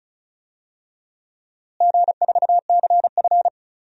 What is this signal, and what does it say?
Morse code, heard as a keyed tone, about 36 wpm, 700 Hz. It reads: G4CF
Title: DXCC-100 Call Signs - 4 Characters - Repeated 3x - 35wpm